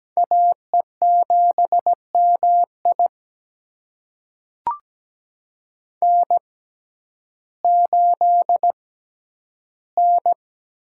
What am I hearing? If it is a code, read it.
Morse code, heard as a keyed tone, about 17 wpm, spaced as for 10 wpm, 700 Hz. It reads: AE7MI N8N
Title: USA Call Signs - 5 Characters - 6x Character Spacing - 17wpm